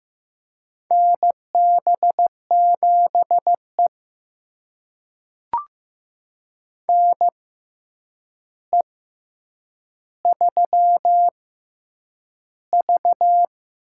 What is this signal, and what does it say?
Morse code, heard as a keyed tone, about 15 wpm, spaced as for 8 wpm, 700 Hz. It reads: NB7E NE3V
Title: USA Call Signs - 4 Characters - 6x Character Spacing - 15wpm